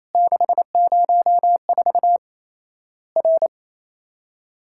Morse code, heard as a keyed tone, about 28 wpm, 700 Hz. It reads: 604 R
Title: Worked All Europe DX Contest - QTC Exchange - 28wpm